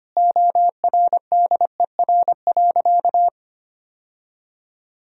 Morse code, heard as a keyed tone, about 25 wpm, 700 Hz. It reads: ORDER.
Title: Sentences from Top 1000 Words - Review Part 2 - 25wpm